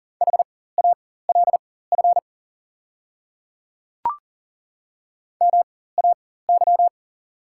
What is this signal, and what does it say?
Morse code, heard as a keyed tone, about 40 wpm, spaced as for 19 wpm, 690 Hz. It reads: HALF MAY
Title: 4x Character Spacing - Top 500 Words - 40wpm